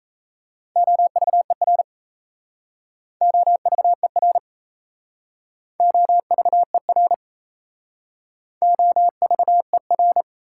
Morse code, heard as a keyed tone, about 38 wpm, 700 Hz. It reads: OVER OVER OVER OVER